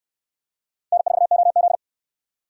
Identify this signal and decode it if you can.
Morse code, heard as a keyed tone, about 62 wpm, 700 Hz. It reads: N4OZ